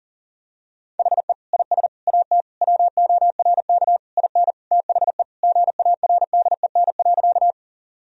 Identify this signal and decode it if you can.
Morse code, heard as a keyed tone, about 40 wpm, 700 Hz. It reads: HE IS AT WORK IN THE GARDEN.
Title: Sentences from Top 1000 Words - Review Part 4 - 40wpm